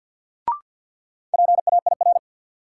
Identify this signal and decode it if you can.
Morse code, heard as a keyed tone, about 53 wpm, 700 Hz. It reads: PAIR
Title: Sets of 3 Words - New Word Drill - 50wpm